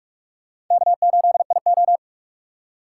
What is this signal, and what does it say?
Morse code, heard as a keyed tone, about 44 wpm, 700 Hz. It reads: K8IO
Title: USA Call Signs - 4 Characters - 4x Character Spacing - 45wpm